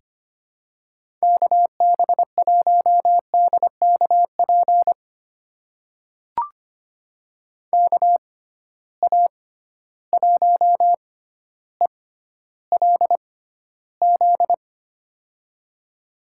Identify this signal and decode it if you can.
Morse code, heard as a keyed tone, about 25 wpm, spaced as for 13 wpm, 700 Hz. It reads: KB1DKP KA1ELZ
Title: USA Call Signs - 6 Characters - 6x Character Spacing - 25wpm